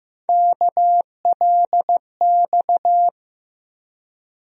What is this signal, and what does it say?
Morse code, heard as a keyed tone, about 15 wpm, 700 Hz. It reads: KLX
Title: USA Call Sign Postfixes - 3 Letters 15wpm